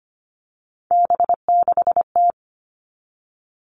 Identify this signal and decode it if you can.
Morse code, heard as a keyed tone, about 25 wpm, 700 Hz. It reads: B6T
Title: Three Challenging Letter-Number Combinations 25wpm